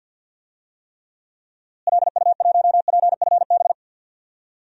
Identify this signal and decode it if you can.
Morse code, heard as a keyed tone, about 47 wpm, 700 Hz. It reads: LU1PFB
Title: DXCC-100 Call Signs - 6 Characters - Repeated 2x - 50wpm